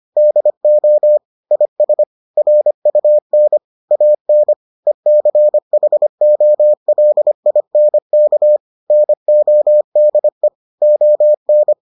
Morse code, heard as a keyed tone, about 25 wpm, 600 Hz. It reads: DO IS RUN AN ECHOLINK NODE ON